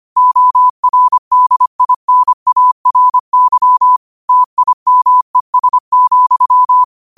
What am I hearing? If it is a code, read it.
Morse code, heard as a keyed tone, about 25 wpm, 1000 Hz. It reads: ORDINARY TIMES,